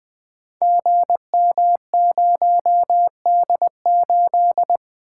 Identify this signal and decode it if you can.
Morse code, heard as a keyed tone, about 20 wpm, 700 Hz. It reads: GM0D8
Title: Code Groups - 5 Letter:Numbers - 20wpm